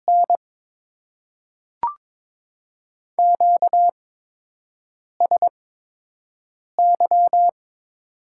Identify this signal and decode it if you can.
Morse code, heard as a keyed tone, about 22 wpm, spaced as for 6 wpm, 690 Hz. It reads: N QSY
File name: Top 90 QSO Elements - 8x Character Spacing - 22wpm